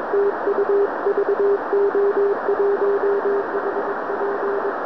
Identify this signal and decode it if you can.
Morse code, heard as a keyed tone, about 21 wpm, 400 Hz, with radio noise. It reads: TU VO1HP